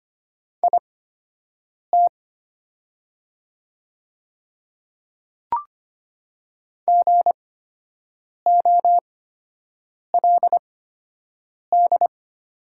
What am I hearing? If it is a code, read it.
Morse code, heard as a keyed tone, about 25 wpm, spaced as for 7 wpm, 700 Hz. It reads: IT GOLD
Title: 8x Character Spacing - Top 500 Words - 25wpm